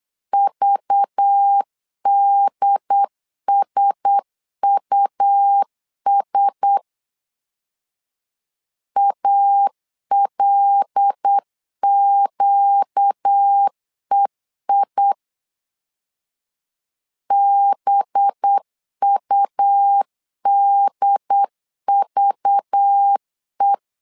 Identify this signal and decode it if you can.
Morse code, heard as a keyed tone, about 8 wpm, 800 Hz. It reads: VDSUS ALQEI BUDVE
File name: LEZIONI / 5 LBVC / 5 LBV 8wpm 2837